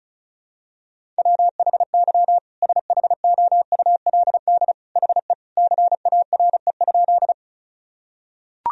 Morse code, heard as a keyed tone, about 36 wpm, 700 Hz. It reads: WHY SHOULD HE CARE?
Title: Sentences from Top 1000 Words - Review Part 1 - 35wpm